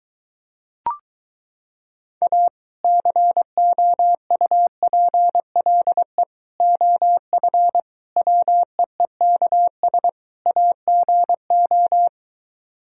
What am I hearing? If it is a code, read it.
Morse code, heard as a keyed tone, about 23 wpm, 700 Hz. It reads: A COUPLE OF WEEKS AGO